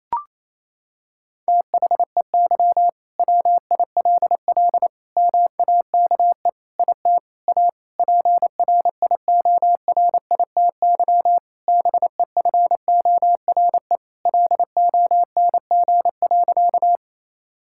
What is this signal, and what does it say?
Morse code, heard as a keyed tone, about 28 wpm, 700 Hz. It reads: THEY WILL MAKE IT A PRIORITY BEFORE LONG.
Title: Sentences from Top 1600 Words 28wpm